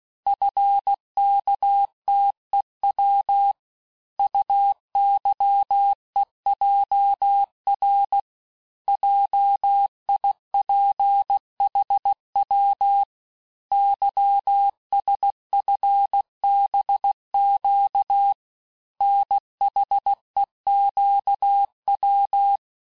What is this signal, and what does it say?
Morse code, heard as a keyed tone, about 16 wpm, 800 Hz. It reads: FKTEW UYEJR JIPHW YSFBQ NHEQW
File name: cours-du-lundi files / cours-1592--teste-chiffre-vitesse-15mpm